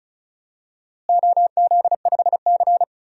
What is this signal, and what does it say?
Morse code, heard as a keyed tone, about 36 wpm, 700 Hz. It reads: OZ5C
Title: DXCC-100 Call Signs - 4 Characters - Repeated 2x - 35wpm